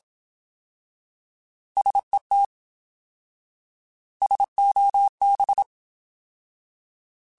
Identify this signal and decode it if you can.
Morse code, heard as a keyed tone, about 27 wpm, 800 Hz. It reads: SET SOB
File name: Alphabetical / Alphabetical mp3s / Ss @25